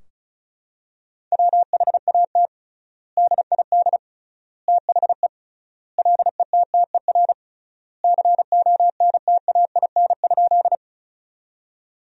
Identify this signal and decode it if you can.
Morse code, heard as a keyed tone, about 36 wpm, 700 Hz. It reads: WHAT DID THE LETTER CONTAIN?
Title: 3x Word-Spacing - Sentences from Top 500 Words - 35wpm